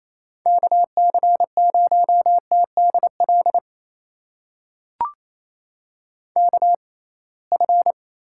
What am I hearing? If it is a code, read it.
Morse code, heard as a keyed tone, about 28 wpm, spaced as for 21 wpm, 700 Hz. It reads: KC0TDL KF
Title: USA Call Signs - 6 Characters - 6x Character Spacing - 28wpm